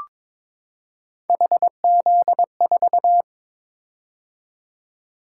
Morse code, heard as a keyed tone, about 22 wpm, 700 Hz. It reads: HZ4
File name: practice / 22wpm / Three Letter-Number Combinations - 22wpm